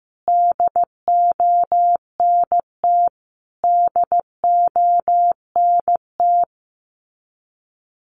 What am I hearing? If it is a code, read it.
Morse code, heard as a keyed tone, about 15 wpm, 700 Hz. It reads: DONT DONT
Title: Top 500 Words Repeated 15wpm